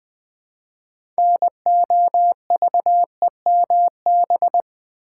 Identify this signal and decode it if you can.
Morse code, heard as a keyed tone, about 20 wpm, 700 Hz. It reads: NOVEMB
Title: Build A Word - 2 Letters At A Time - 8-Letter Words - 20wpm